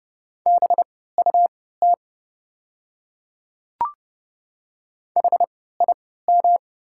Morse code, heard as a keyed tone, about 30 wpm, spaced as for 17 wpm, 700 Hz. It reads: BUT HIM
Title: 3x Character Spacing - Top 100 Words - 30wpm